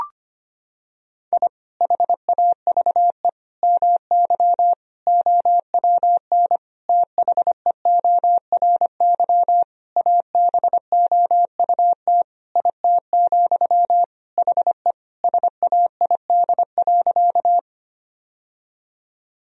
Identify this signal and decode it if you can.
Morse code, heard as a keyed tone, about 25 wpm, 700 Hz. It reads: I HAVE MY OWN THEORY ABOUT IT, HE SAID.
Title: Sentences from Top 1000 Words - Review Part 3 - 25wpm